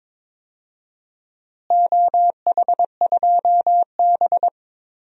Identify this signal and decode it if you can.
Morse code, heard as a keyed tone, about 22 wpm, 700 Hz. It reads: OH2B
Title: DXCC-100 Call Signs - 4 Characters - Repeated 2x - 22wpm